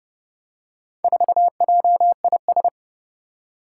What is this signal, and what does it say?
Morse code, heard as a keyed tone, about 28 wpm, 700 Hz. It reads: 4JIS
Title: Four Challenging Letter-Number Combinations 30wpm